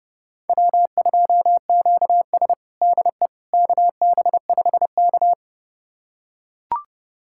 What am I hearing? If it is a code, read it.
Morse code, heard as a keyed tone, about 30 wpm, 700 Hz. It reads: W2QS DE KB5K
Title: USA Callsign DE Callsign - 4 Characters - 30wpm